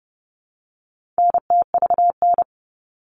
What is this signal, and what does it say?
Morse code, heard as a keyed tone, about 30 wpm, 700 Hz. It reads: NTVN
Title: All Cut Numbers - 4-Digit 30wpm